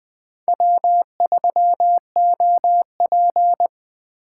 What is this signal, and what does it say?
Morse code, heard as a keyed tone, about 21 wpm, 700 Hz. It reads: W3OP
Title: Speed Racing - USA Call Signs 17wpm